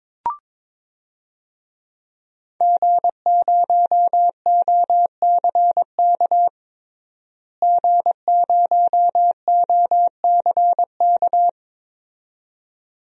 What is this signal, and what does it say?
Morse code, heard as a keyed tone, about 22 wpm, 700 Hz. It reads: G0OCK G0OCK
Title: DXCC-100 Call Signs - 5 Characters - Repeated 2x - 22wpm